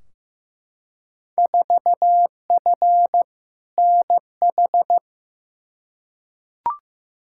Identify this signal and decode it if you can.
Morse code, heard as a keyed tone, about 15 wpm, 700 Hz. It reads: 4F NH
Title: Field Day Exchanges Encore - 15wpm